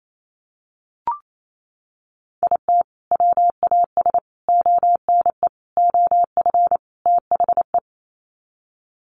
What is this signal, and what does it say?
Morse code, heard as a keyed tone, about 28 wpm, 700 Hz. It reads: IT WAS ONE OF THE
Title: Sets of 5 Words 28wpm